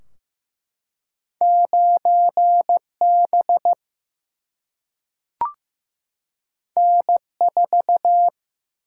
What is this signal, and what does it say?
Morse code, heard as a keyed tone, about 15 wpm, 700 Hz. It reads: 9B N4
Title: Single-Letter Single-Number 15wpm